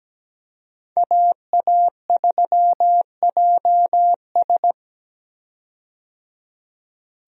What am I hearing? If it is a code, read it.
Morse code, heard as a keyed tone, about 17 wpm, 700 Hz. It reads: AA3JS
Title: USA Call Signs - 5 Characters - 17wpm